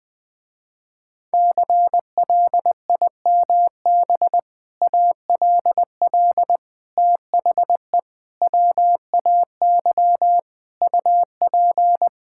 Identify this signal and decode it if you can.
Morse code, heard as a keyed tone, about 20 wpm, 700 Hz. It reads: CLIMB ALL THE WAY UP
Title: Sentences with 6-Word Phrases - Spoken Beginning - 20wpm